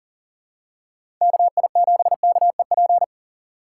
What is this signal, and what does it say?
Morse code, heard as a keyed tone, about 40 wpm, 700 Hz. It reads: KI7KEP
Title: USA Call Signs - 6x Character Spacing - 40wpm